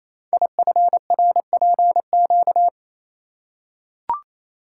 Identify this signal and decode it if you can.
Morse code, heard as a keyed tone, about 27 wpm, 700 Hz. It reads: IFRPQ
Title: Code Groups - 5 Letter:Numbers - 28wpm